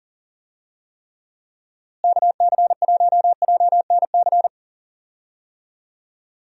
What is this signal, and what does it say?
Morse code, heard as a keyed tone, about 40 wpm, 700 Hz. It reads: KC1JNC
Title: USA Call Signs - 6 Characters - Repeated 2x - 40wpm